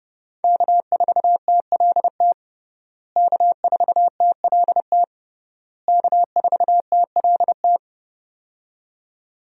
Morse code, heard as a keyed tone, about 32 wpm, 700 Hz. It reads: K4TLT K4TLT K4TLT
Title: USA Call Signs - 5 Characters - Repeated 3x - 30wpm